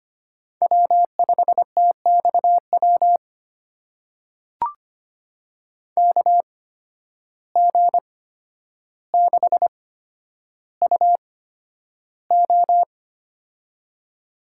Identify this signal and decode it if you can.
Morse code, heard as a keyed tone, about 25 wpm, spaced as for 11 wpm, 700 Hz. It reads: W5TXW KG6UO
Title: USA Call Signs - 5 Characters - 8x Character Spacing - 25wpm